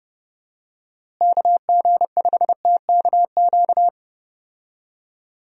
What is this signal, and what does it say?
Morse code, heard as a keyed tone, about 30 wpm, 700 Hz. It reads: KG5TKQ
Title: USA Call Signs - 6 Characters - Repeated 2x - 30wpm